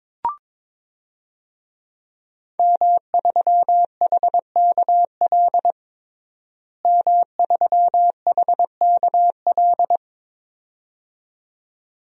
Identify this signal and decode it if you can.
Morse code, heard as a keyed tone, about 22 wpm, 700 Hz. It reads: M3HKL M3HKL
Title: DXCC-100 Call Signs - 5 Characters - Repeated 2x - 22wpm